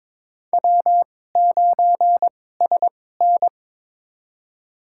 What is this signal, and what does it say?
Morse code, heard as a keyed tone, about 22 wpm, spaced as for 16 wpm, 700 Hz. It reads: W9SN
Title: USA Call Signs - 4 Characters - 2x Character Spacing - 22wpm